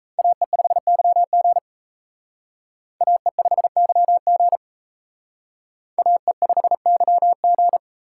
Morse code, heard as a keyed tone, about 35 wpm, 700 Hz. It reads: AE5YG AE5YG AE5YG